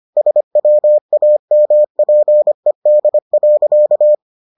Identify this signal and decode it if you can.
Morse code, heard as a keyed tone, about 25 wpm, 600 Hz. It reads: SWAMPED.